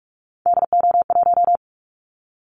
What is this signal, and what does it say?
Morse code, heard as a keyed tone, about 44 wpm, 700 Hz. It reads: DO1
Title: Callsign Prefixes for Top 25 DXCC - Repeated 3x - 45wpm